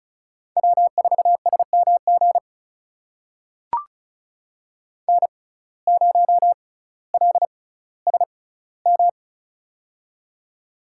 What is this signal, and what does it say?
Morse code, heard as a keyed tone, about 36 wpm, spaced as for 18 wpm, 700 Hz. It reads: W4SMG N0LSM
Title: USA Call Signs - 5 Characters - 6x Character Spacing - 35wpm